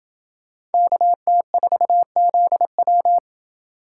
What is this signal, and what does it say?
Morse code, heard as a keyed tone, about 24 wpm, 700 Hz. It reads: KT4ZW
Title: Speed Racing - USA 2x2 Call Signs 20wpm